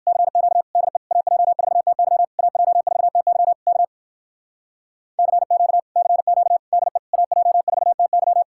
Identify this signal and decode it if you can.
Morse code, heard as a keyed tone, about 62 wpm, 700 Hz. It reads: CQ DE AJ4TY AJ4TY K CQ CQ DE AJ4TY